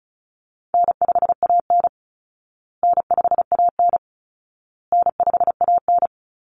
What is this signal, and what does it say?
Morse code, heard as a keyed tone, about 34 wpm, 700 Hz. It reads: N5AN N5AN N5AN
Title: Call Signs Repeated 2x - 35wpm